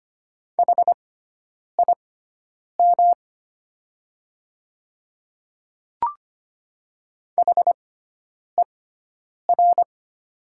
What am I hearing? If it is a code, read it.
Morse code, heard as a keyed tone, about 25 wpm, spaced as for 9 wpm, 700 Hz. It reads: HIM HER
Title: 6x Character Spacing - Top 200 Words - 25wpm